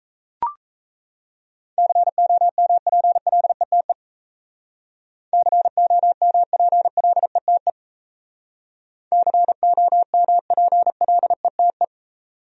Morse code, heard as a keyed tone, about 34 wpm, 700 Hz. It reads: COMPLETE COMPLETE COMPLETE
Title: Speed-Racing Top 500 Words 28wpm